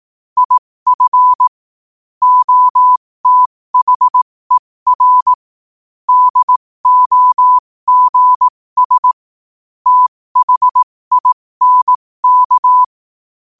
Morse code, heard as a keyed tone, about 18 wpm, spaced as for 15 wpm, 1000 Hz. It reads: IF OTHER DOGS THINK
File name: qotdfiles / 15wpm